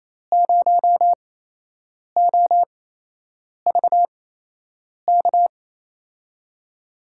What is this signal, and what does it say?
Morse code, heard as a keyed tone, about 28 wpm, spaced as for 8 wpm, 700 Hz. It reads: 0OVK
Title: USA Call Signs - 5 Characters - 8x Character Spacing - 28wpm